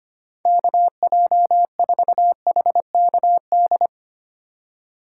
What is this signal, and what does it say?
Morse code, heard as a keyed tone, about 32 wpm, 700 Hz. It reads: KJ4HKD